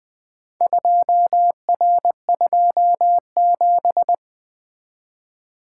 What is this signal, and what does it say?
Morse code, heard as a keyed tone, about 20 wpm, 700 Hz. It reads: 2R27
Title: Four Letter-Number Combinations - 20wpm